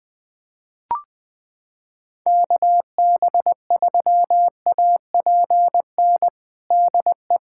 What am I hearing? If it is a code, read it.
Morse code, heard as a keyed tone, about 20 wpm, 700 Hz. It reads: KB3APN DE
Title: USA Callsign DE Callsign - 6 Characters - 20wpm